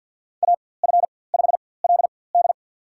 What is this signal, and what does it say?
Morse code, heard as a keyed tone, about 49 wpm, spaced as for 23 wpm, 700 Hz. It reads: AF5LD
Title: USA Call Signs - 5 Characters - 4x Character Spacing - 50wpm